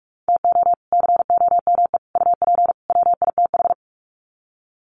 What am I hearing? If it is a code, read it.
Morse code, heard as a keyed tone, about 44 wpm, 700 Hz. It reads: TO COME UP WITH
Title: Sets of 4 Words 45wpm